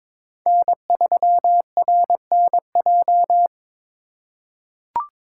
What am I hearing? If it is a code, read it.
Morse code, heard as a keyed tone, about 21 wpm, 700 Hz. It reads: N3RNJ
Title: USA Call Signs - 5 Characters - 2x Character Spacing - 22wpm